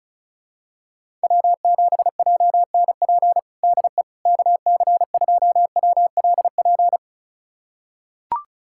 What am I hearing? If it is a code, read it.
Morse code, heard as a keyed tone, about 34 wpm, 700 Hz. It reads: W7JNP DE KC2WLP
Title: USA Callsign DE Callsign 35wpm